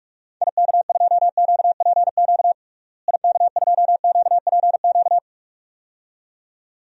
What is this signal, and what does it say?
Morse code, heard as a keyed tone, about 46 wpm, 700 Hz. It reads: IK2QPQ IK2QPQ
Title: DXCC-100 Call Signs - 6 Characters - Repeated 2x - 45wpm